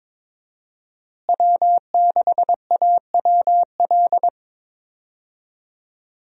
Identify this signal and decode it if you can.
Morse code, heard as a keyed tone, about 22 wpm, 700 Hz. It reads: W6AWL
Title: USA Call Signs - 5 Characters - Repeated 2x - 22wpm